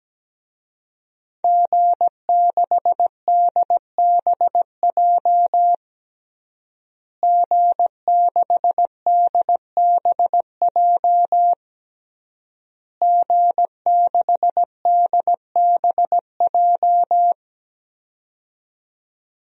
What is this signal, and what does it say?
Morse code, heard as a keyed tone, about 17 wpm, 700 Hz. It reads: G6DBJ G6DBJ G6DBJ
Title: DXCC-100 Call Signs - 5 Characters - Repeated 3x - 17wpm